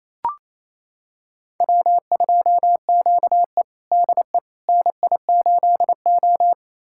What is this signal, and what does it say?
Morse code, heard as a keyed tone, about 28 wpm, 700 Hz. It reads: W2QE DE NI8O
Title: USA Callsign DE Callsign - 4 Characters - 28wpm